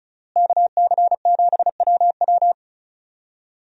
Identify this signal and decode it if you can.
Morse code, heard as a keyed tone, about 36 wpm, 700 Hz. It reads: KC7WW
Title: USA Call Signs - 5 Characters - Repeated 3x - 35wpm